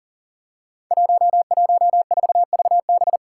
Morse code, heard as a keyed tone, about 41 wpm, 700 Hz. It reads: 114VB
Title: Five Challenging Letter-Number Combinations 40wpm